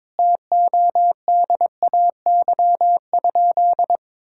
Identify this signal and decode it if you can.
Morse code, heard as a keyed tone, about 21 wpm, 700 Hz. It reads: TODAY?
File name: practice / 22wpm / Sentences with 4-Word Phrases - 22wpm